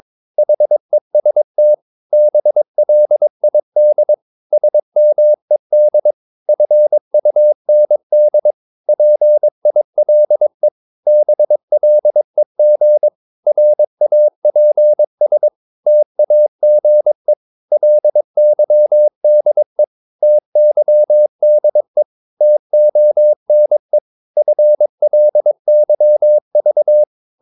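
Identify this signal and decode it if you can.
Morse code, heard as a keyed tone, about 22 wpm, 600 Hz. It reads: HEST BLID SMED FUND PILE BLEG RAPS TAGE LYDE TYDE TONE FLYV
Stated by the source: Morse practice of short Danish words at 22 wpm, from CW med Gnister.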